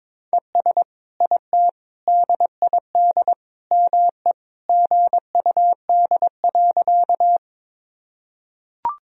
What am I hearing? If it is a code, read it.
Morse code, heard as a keyed tone, about 21 wpm, 700 Hz. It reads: ES IT DID ME GUD.
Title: Sentences from Top 200 Words and Abbreviations 22wpm